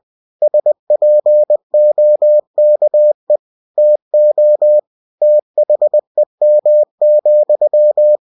Morse code, heard as a keyed tone, about 20 wpm, 600 Hz. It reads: SPOKE TO THEM,
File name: cw-books / winnie-the-pooh / winnie-20wpm-page-0106